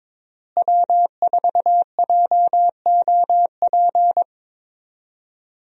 Morse code, heard as a keyed tone, about 22 wpm, 700 Hz. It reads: W4JOP
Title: USA Call Signs - 5 Characters - Repeated 3x - 22wpm